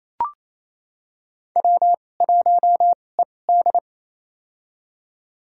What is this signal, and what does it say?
Morse code, heard as a keyed tone, about 28 wpm, spaced as for 20 wpm, 700 Hz. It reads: W1ED
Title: USA Call Signs - 4 Characters - 2x Character Spacing - 28wpm